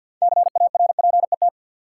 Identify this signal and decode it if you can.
Morse code, heard as a keyed tone, about 52 wpm, 700 Hz. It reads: CARPET
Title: Six Letter Words - 50wpm